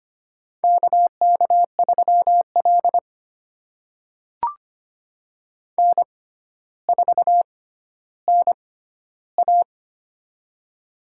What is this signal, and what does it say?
Morse code, heard as a keyed tone, about 25 wpm, spaced as for 13 wpm, 700 Hz. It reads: KK3L N4NA
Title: USA Call Signs - 4 Characters - 6x Character Spacing - 25wpm